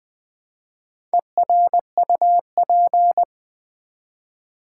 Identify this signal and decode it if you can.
Morse code, heard as a keyed tone, about 20 wpm, 700 Hz. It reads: ERUP
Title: Build A Word - 1 Letter At A Time - 5-Letter Words - 20wpm